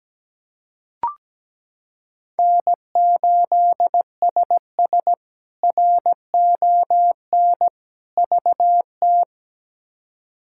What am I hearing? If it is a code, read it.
Morse code, heard as a keyed tone, about 17 wpm, 700 Hz. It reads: N8SS RON VT
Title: K1USN SST QSO Practice - 17wpm